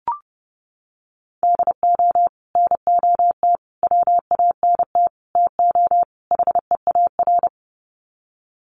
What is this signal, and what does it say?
Morse code, heard as a keyed tone, about 30 wpm, 700 Hz. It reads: DO NOT WANT TO HEAR
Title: Sets of 5 Words 30wpm